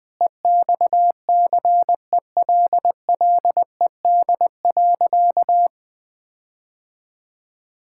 Morse code, heard as a keyed tone, about 20 wpm, 700 Hz. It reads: EXCELLED.
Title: Sentences with 6-Word Phrases - Spoken Beginning - 20wpm